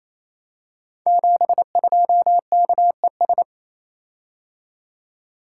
Morse code, heard as a keyed tone, about 28 wpm, 700 Hz. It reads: 72KES
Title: Code Groups - 5 Letter:Numbers - 28wpm